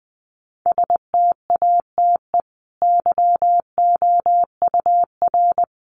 Morse code, heard as a keyed tone, about 19 wpm, 700 Hz. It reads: STATE YOUR
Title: Sentences from Top 700 Words 20wpm